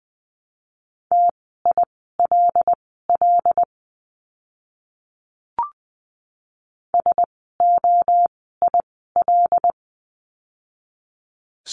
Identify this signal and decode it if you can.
Morse code, heard as a keyed tone, about 20 wpm, spaced as for 14 wpm, 700 Hz. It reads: TILL SOIL
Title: 075 Lesson 13f - Letter L - Words 20wpm